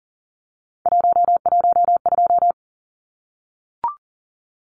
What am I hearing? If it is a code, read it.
Morse code, heard as a keyed tone, about 40 wpm, 700 Hz. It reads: 112
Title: Numbers - 3-Digit 40wpm